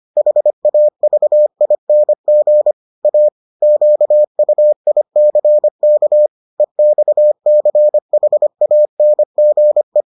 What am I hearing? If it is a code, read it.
Morse code, heard as a keyed tone, about 25 wpm, 600 Hz. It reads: HAVING A QUICK EXCHANGE